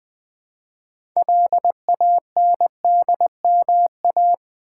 Morse code, heard as a keyed tone, about 20 wpm, 700 Hz. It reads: LANDMA
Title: Build A Word - 2 Letters At A Time - 8-Letter Words - 20wpm